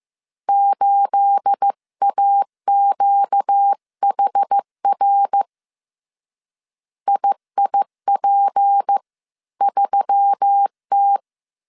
Morse code, heard as a keyed tone, about 15 wpm, 800 Hz. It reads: 8AQHR IIP3T
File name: CW / DET / 1 9 1500C S A 14wpm 3222